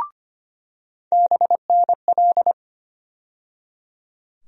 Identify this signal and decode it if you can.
Morse code, heard as a keyed tone, about 25 wpm, 700 Hz. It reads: BNL
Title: USA Call Sign Postfixes - 3 Letters 25wpm